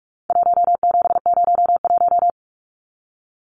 Morse code, heard as a keyed tone, about 44 wpm, 700 Hz. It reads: 1701
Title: Numbers - 4-Digit 45wpm